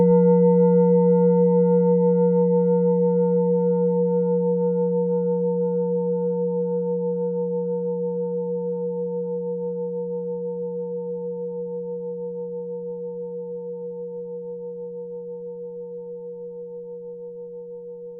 There is no code.